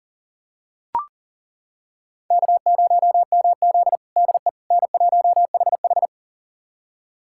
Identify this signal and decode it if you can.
Morse code, heard as a keyed tone, about 37 wpm, 700 Hz. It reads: K0MZ DE N1HH
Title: USA Callsign DE Callsign - 4 Characters - 40wpm